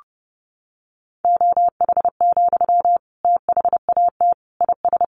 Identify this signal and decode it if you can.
Morse code, heard as a keyed tone, about 30 wpm, 700 Hz. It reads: OH, THAT IS
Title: Sentences from Top 500 Words 30wpm